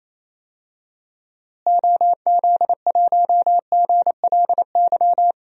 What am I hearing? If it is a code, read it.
Morse code, heard as a keyed tone, about 28 wpm, 700 Hz. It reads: OZ1GLY
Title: DXCC-100 Call Signs - 6 Characters - Repeated 2x - 28wpm